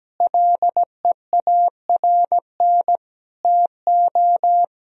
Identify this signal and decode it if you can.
Morse code, heard as a keyed tone, about 17 wpm, 700 Hz. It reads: LEARN TO
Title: Sentences from Top 1000 Words - Review Part 1 - 17wpm